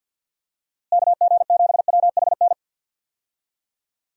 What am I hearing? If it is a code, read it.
Morse code, heard as a keyed tone, about 51 wpm, 700 Hz. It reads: KG7WHN
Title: USA Call Signs - 6 Characters - Repeated 2x - 50wpm